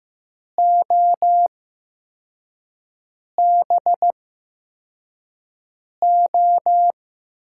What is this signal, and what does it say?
Morse code, heard as a keyed tone, about 15 wpm, spaced as for 4 wpm, 700 Hz. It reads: OBO
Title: USA Call Signs - 8x Character Spacing - 15wpm